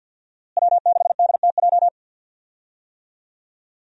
Morse code, heard as a keyed tone, about 47 wpm, 700 Hz. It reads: W6DTJ
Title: USA Call Signs - 5 Characters - Repeated 2x - 50wpm